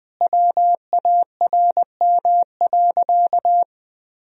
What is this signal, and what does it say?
Morse code, heard as a keyed tone, about 20 wpm, 700 Hz. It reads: WARM.
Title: Sentences from Top 1000 Words - Review Part 2 - 20wpm